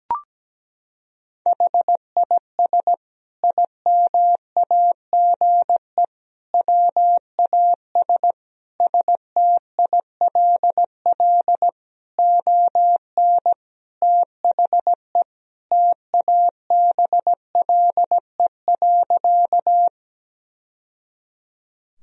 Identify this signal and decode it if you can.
Morse code, heard as a keyed tone, about 17 wpm, 700 Hz. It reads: HIS IMAGE WAS STILL ON THE TABLE.
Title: Sentences from Top 1000 Words - Review Part 4 - 17wpm